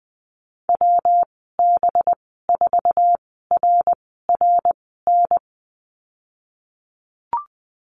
Morse code, heard as a keyed tone, about 20 wpm, spaced as for 15 wpm, 700 Hz. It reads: WB4RRN
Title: 193 Lesson 31g - Letter v - Call Signs 20wpm